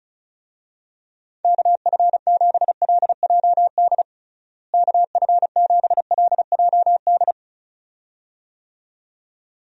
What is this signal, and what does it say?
Morse code, heard as a keyed tone, about 34 wpm, 700 Hz. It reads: KF7LJD KF7LJD
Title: USA Call Signs - 6 Characters - Repeated 2x - 35wpm